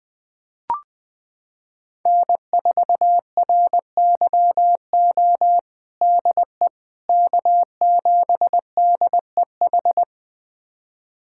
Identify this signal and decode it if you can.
Morse code, heard as a keyed tone, about 21 wpm, 700 Hz. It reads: N4RYO DE K7DEH
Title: USA Callsign DE Callsign - 5 Characters - 20wpm